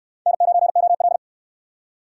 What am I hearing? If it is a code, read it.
Morse code, heard as a keyed tone, about 69 wpm, 700 Hz. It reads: N9OL